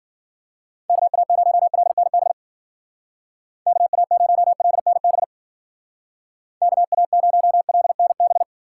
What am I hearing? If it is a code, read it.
Morse code, heard as a keyed tone, about 51 wpm, 700 Hz. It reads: KA0LNB KA0LNB KA0LNB